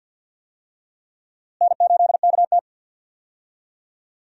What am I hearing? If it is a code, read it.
Morse code, heard as a keyed tone, about 52 wpm, 700 Hz. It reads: N8KT
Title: USA Call Signs - 4 Characters - Repeated 2x - 50wpm